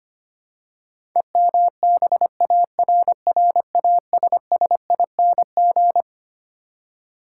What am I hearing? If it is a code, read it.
Morse code, heard as a keyed tone, about 25 wpm, 700 Hz. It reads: EMBARRASSING
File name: practice / 25wpm / Twelve Letter Words - 25wpm